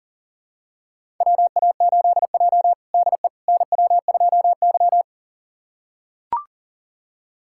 Morse code, heard as a keyed tone, about 40 wpm, 700 Hz. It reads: WA8J DE NW2Y
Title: USA Callsign DE Callsign - 4 Characters - 40wpm